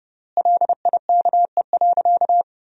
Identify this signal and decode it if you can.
Morse code, heard as a keyed tone, about 30 wpm, 700 Hz. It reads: LIKE.
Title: 4x Word-Spacing - Sentences from Top 500 Words - 30wpm